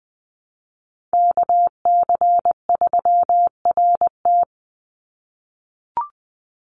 Morse code, heard as a keyed tone, about 20 wpm, 700 Hz. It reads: KC3RT
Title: NA SOTA Chaser Callsigns 20wpm